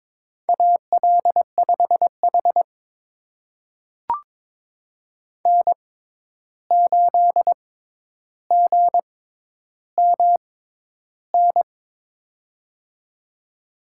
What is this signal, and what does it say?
Morse code, heard as a keyed tone, about 22 wpm, spaced as for 11 wpm, 700 Hz. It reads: AL5H N8GMN
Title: USA Call Signs - 6x Character Spacing - 22wpm